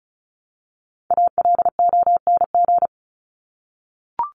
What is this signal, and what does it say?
Morse code, heard as a keyed tone, about 34 wpm, 700 Hz. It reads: ALONG
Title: Top 501-750 Words 35wpm